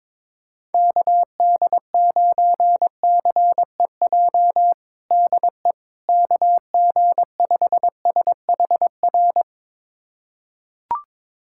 Morse code, heard as a keyed tone, about 22 wpm, 700 Hz. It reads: KD9CEJ DE KG5SHR
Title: USA Callsign DE Callsign - 6 Characters - 22wpm